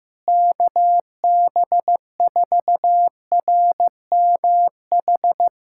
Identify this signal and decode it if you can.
Morse code, heard as a keyed tone, about 15 wpm, 700 Hz. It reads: KB4RMH
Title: USA Callsign DE Callsign 15wpm